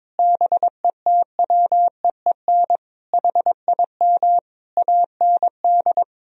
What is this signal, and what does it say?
Morse code, heard as a keyed tone, about 22 wpm, 700 Hz. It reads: BETWEEN HIM AND
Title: Sentences from Top 1000 Words - Review Part 1 - 22wpm